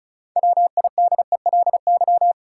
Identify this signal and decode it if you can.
Morse code, heard as a keyed tone, about 36 wpm, 700 Hz. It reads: WIDELY
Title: Sentences from Top 1200 Words - New Word Drill - 35wpm